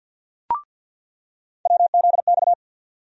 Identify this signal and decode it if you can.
Morse code, heard as a keyed tone, about 48 wpm, 700 Hz. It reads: WZX
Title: USA Call Sign Postfixes - 3 Letters 50wpm